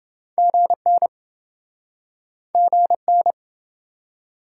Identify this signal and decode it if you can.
Morse code, heard as a keyed tone, about 27 wpm, 700 Hz. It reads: GN GN